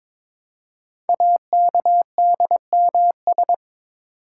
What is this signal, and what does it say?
Morse code, heard as a keyed tone, about 22 wpm, 700 Hz. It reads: AKDMS
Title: Code Groups - 5 Letters - 22wpm